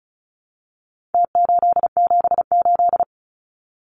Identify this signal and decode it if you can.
Morse code, heard as a keyed tone, about 34 wpm, 700 Hz. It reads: T878
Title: Numbers - Contest Style - 4-Digit 35wpm